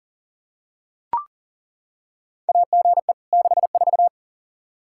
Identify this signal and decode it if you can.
Morse code, heard as a keyed tone, about 40 wpm, 700 Hz. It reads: AGE 64
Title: Practice QSO - Exchange 3 - More Info - 40wpm